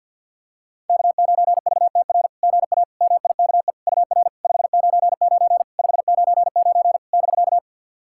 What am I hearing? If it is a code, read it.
Morse code, heard as a keyed tone, about 50 wpm, 690 Hz. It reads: K9VTR GA MIKE UR 599 599 <BK>
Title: SOTA QSO Practice 50wpm